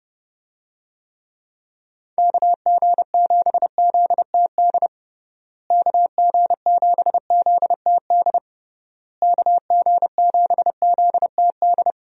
Morse code, heard as a keyed tone, about 30 wpm, 700 Hz. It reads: KG7ZTD KG7ZTD KG7ZTD
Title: USA Call Signs - 6 Characters - Repeated 3x - 30wpm